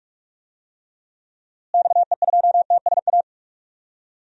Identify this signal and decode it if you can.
Morse code, heard as a keyed tone, about 43 wpm, 700 Hz. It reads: XE2TSA
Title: DXCC-100 Call Signs - 6 Characters - Repeated 2x - 45wpm